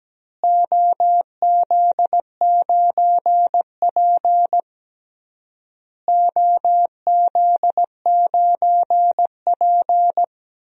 Morse code, heard as a keyed tone, about 17 wpm, 700 Hz. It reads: OZ9P OZ9P
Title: DXCC-100 Call Signs - 4 Characters - Repeated 3x - 17wpm